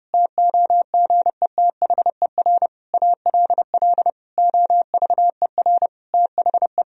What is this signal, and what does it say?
Morse code, heard as a keyed tone, about 30 wpm, 700 Hz. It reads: TOGETHER ALL OVER THE
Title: Sentences from Top 1000 Words - Review Part 2 - 30wpm